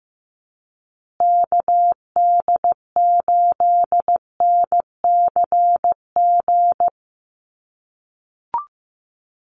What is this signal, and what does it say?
Morse code, heard as a keyed tone, about 15 wpm, 700 Hz. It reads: KD8NCG
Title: USA Call Signs Repeated 3x 15wpm